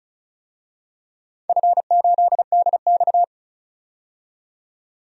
Morse code, heard as a keyed tone, about 36 wpm, 700 Hz. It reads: F8DX
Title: DXCC-100 Call Signs - 4 Characters - 35wpm